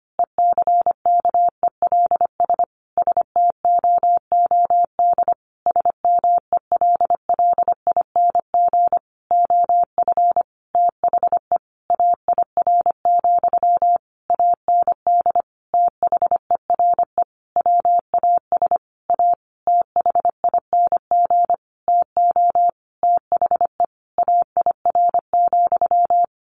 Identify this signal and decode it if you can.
Morse code, heard as a keyed tone, about 25 wpm, 700 Hz. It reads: ECKELS STOOD SMELLING OF THE AIR, AND THERE WAS A THING TO THE AIR,
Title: A Sound Of Thunder 25wpm